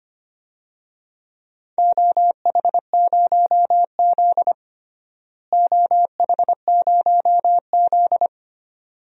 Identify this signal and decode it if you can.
Morse code, heard as a keyed tone, about 25 wpm, 700 Hz. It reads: OH0Z OH0Z
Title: DXCC-100 Call Signs - 4 Characters - Repeated 2x - 25wpm